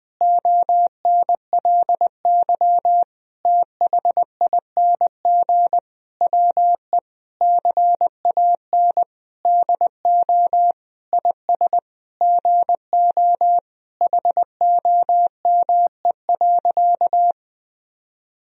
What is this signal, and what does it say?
Morse code, heard as a keyed tone, about 20 wpm, 700 Hz. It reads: ONLY THING WE CAN DO IS GO HOME.
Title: Sentences from Top 200 Words and Abbreviations 20wpm